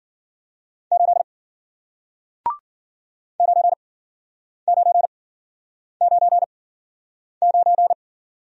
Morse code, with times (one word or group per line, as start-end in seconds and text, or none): 0.9-1.2 8
3.4-7.9 9999